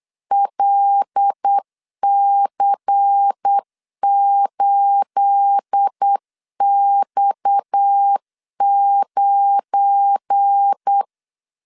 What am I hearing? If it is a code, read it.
Morse code, heard as a keyed tone, about 8 wpm, 800 Hz. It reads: LC8X9
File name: CW / DET / 1 9 1500C S C 8wpm 5653